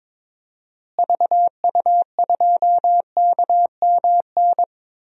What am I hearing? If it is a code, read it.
Morse code, heard as a keyed tone, about 22 wpm, 700 Hz. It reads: VU2KMN
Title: DXCC-100 Call Signs - 6 Characters - Repeated 2x - 22wpm